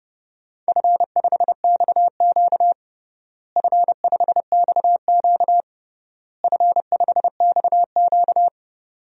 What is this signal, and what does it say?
Morse code, heard as a keyed tone, about 30 wpm, 700 Hz. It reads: F5XQ F5XQ F5XQ
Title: DXCC-100 Call Signs - 4 Characters - Repeated 3x - 30wpm